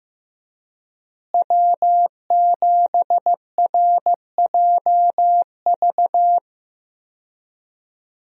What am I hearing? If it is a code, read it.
Morse code, heard as a keyed tone, about 15 wpm, 700 Hz. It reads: W7RJV
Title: USA Call Signs - 5 Characters - Repeated 3x - 15wpm